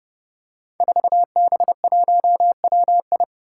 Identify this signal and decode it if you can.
Morse code, heard as a keyed tone, about 29 wpm, 700 Hz. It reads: 4B1WI
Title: Five Challenging Letter-Number Combinations 30wpm